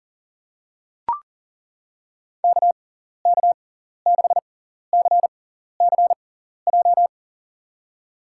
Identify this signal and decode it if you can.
Morse code, heard as a keyed tone, about 40 wpm, spaced as for 14 wpm, 700 Hz. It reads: KK6CCJ
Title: USA Call Signs - 6 Characters - 6x Character Spacing - 40wpm